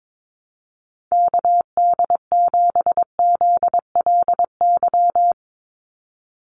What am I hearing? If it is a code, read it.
Morse code, heard as a keyed tone, about 22 wpm, 700 Hz. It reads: KD7ZLY
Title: USA Call Signs 22wpm